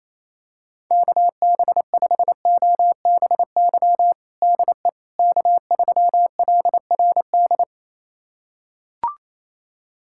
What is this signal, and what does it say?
Morse code, heard as a keyed tone, about 28 wpm, 700 Hz. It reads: KB5OBY DE K3LRD
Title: USA Callsign DE Callsign 28wpm